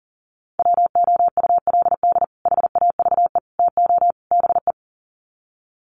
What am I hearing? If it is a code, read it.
Morse code, heard as a keyed tone, about 37 wpm, 700 Hz. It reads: WOULD HAVE TO BE
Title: Sets of 4 Words 40wpm